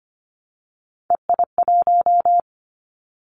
Morse code, heard as a keyed tone, about 25 wpm, 700 Hz. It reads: EI1
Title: Callsign Prefixes for Top 50 DXCC - 25wpm